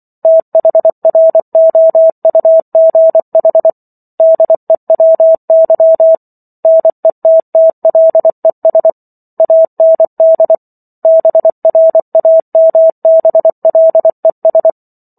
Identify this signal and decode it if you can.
Morse code, heard as a keyed tone, about 24 wpm, 650 Hz. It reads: THROUGH DEWY NETTLES AND BRAMBLES